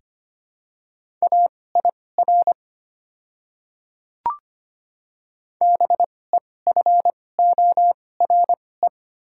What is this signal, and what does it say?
Morse code, heard as a keyed tone, about 25 wpm, spaced as for 18 wpm, 690 Hz. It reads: AIR BEFORE
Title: 2x Character Spacing - Top 300 Words - 25wpm